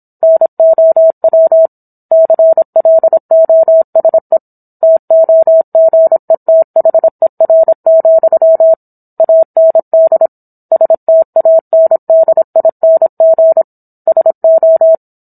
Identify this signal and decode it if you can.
Morse code, heard as a keyed tone, about 27 wpm, 650 Hz. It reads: NOW CLOSE TOGETHER, AND STANDING SO